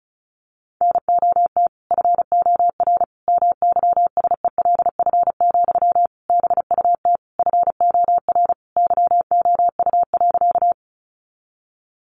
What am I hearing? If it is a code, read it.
Morse code, heard as a keyed tone, about 34 wpm, 700 Hz. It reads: NOT FOR MYSELF, BUT FOR YOU.
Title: Sentences from Top 1300 Words 35wpm